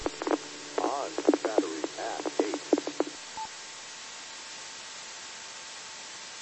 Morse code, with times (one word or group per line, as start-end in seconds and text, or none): none